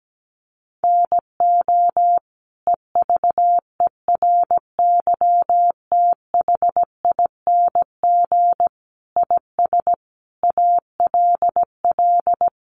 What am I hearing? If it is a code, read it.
Morse code, heard as a keyed tone, about 17 wpm, 700 Hz. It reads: NO EVERYTHING IS ALL
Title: Sentences from Top 700 Words 17wpm